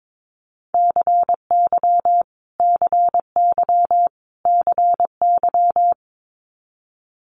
Morse code, heard as a keyed tone, about 22 wpm, 700 Hz. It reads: CY CY CY
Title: Multi-letter Phonemes Repeated 3x 22wpm